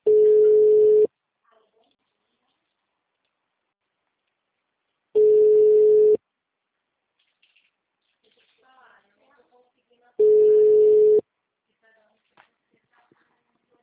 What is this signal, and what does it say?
Ringback tone (Europe, 425 Hz), with speech.